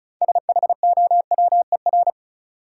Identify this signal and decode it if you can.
Morse code, heard as a keyed tone, about 34 wpm, 700 Hz. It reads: SHOWER
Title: Six Letter Words - 35wpm